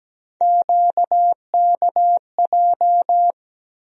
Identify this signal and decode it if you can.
Morse code, heard as a keyed tone, about 17 wpm, 700 Hz. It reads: QKJ
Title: USA Call Sign Postfixes - 3 Letters 17wpm